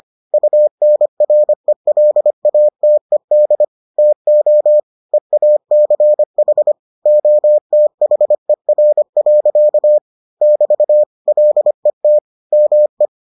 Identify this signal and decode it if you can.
Morse code, heard as a keyed tone, about 25 wpm, 600 Hz. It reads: UNRELATED TO EACH OTHER. <BT> LET ME